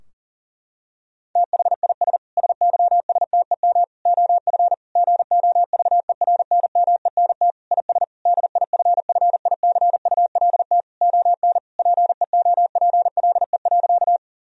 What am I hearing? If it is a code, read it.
Morse code, heard as a keyed tone, about 40 wpm, 700 Hz. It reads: THIS SYSTEM OF GOVERNMENT IS DIFFICULT ON PEOPLE.
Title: Sentences from Top 1000 Words - Review Part 3 - 40wpm